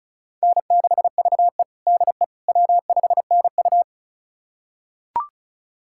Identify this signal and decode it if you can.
Morse code, heard as a keyed tone, about 36 wpm, 700 Hz. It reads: N6VE DE W5NU
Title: USA Callsign DE Callsign - 4 Characters - 35wpm